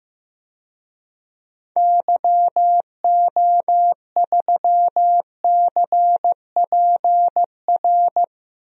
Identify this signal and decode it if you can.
Morse code, heard as a keyed tone, about 15 wpm, 700 Hz. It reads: YO3CPR
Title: DXCC-100 Call Signs - 6 Characters - Repeated 2x - 15wpm